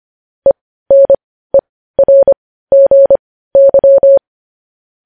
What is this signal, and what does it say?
Morse code, heard as a keyed tone, about 25 wpm, spaced as for 15 wpm, 550 Hz. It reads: ENERGY
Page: news headlines in morse code at 15 wpm